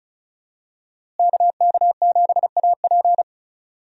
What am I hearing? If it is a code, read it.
Morse code, heard as a keyed tone, about 35 wpm, 700 Hz. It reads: KK7AP
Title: USA Call Signs - 5 Characters - Repeated 3x - 35wpm